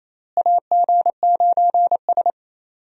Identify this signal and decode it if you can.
Morse code, heard as a keyed tone, about 28 wpm, 700 Hz. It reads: AG9S
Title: USA Call Signs - 4 Characters - Repeated 2x - 28wpm